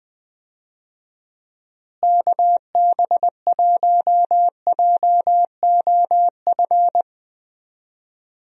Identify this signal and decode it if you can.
Morse code, heard as a keyed tone, about 20 wpm, 700 Hz. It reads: KB1JOF
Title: USA Call Signs - 6 Characters - Repeated 2x - 20wpm